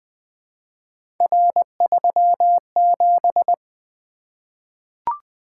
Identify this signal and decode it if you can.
Morse code, heard as a keyed tone, about 20 wpm, 700 Hz. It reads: R37
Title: Three Letter-Number Combinations - 20wpm